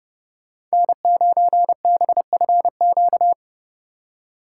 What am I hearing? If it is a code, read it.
Morse code, heard as a keyed tone, about 36 wpm, 700 Hz. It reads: N9BFQ
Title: Speed Racing - USA 1x3 Call Signs 30wpm